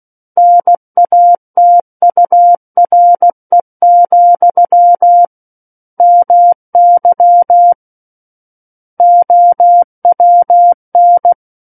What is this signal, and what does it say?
Morse code, heard as a keyed tone, about 16 wpm, 700 Hz. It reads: NATURE, MY OWN
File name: Book One / 07-Chapter 7 - 16 WPM